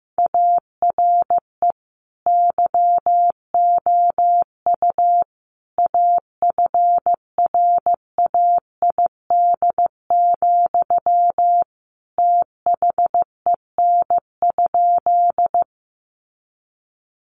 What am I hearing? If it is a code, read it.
Morse code, heard as a keyed tone, about 15 wpm, 700 Hz. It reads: ARE YOU AFRAID, THEN?
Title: Sentences from Top 1500 Words 15wpm